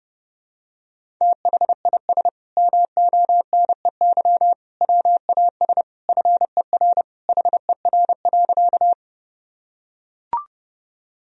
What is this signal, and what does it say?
Morse code, heard as a keyed tone, about 31 wpm, 700 Hz. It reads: THIS MONEY WAS FER HER.
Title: Sentences from Top 400 Words and Abbreviations 30wpm